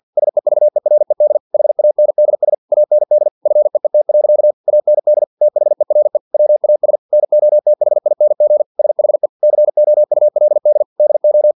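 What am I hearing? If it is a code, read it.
Morse code, heard as a keyed tone, about 50 wpm, 600 Hz. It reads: SEVERED HANDS AND FEET. AND THERE WAS NOTHING SHE COULD DO
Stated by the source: Morse practice text from an English-language novel at 49 wpm.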